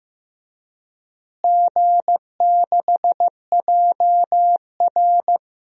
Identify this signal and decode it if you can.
Morse code, heard as a keyed tone, about 15 wpm, 700 Hz. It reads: G6JR
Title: DXCC-100 Call Signs - 4 Characters - Repeated 3x - 15wpm